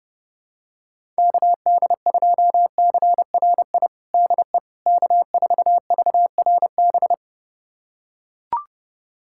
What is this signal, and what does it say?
Morse code, heard as a keyed tone, about 32 wpm, 700 Hz. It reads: KD2CRI DE K4VRB
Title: USA Callsign DE Callsign 30wpm